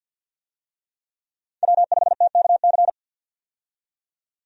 Morse code, heard as a keyed tone, about 48 wpm, 700 Hz. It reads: W5TKC
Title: USA Call Signs - 5 Characters - 50wpm